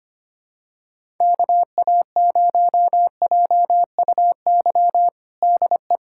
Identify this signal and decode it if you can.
Morse code, heard as a keyed tone, about 25 wpm, 700 Hz. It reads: KA0JUY DE
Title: USA Callsign DE Callsign 25wpm